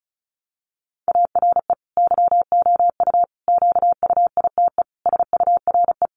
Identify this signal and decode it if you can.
Morse code, heard as a keyed tone, about 34 wpm, 700 Hz. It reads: ARE YOU QUITE SURE
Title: Sentences from Top 700 Words 35wpm